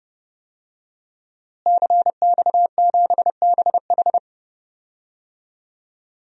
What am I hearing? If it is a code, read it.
Morse code, heard as a keyed tone, about 30 wpm, 700 Hz. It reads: CX7BH
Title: DXCC-100 Call Signs - 5 Characters - Repeated 2x - 30wpm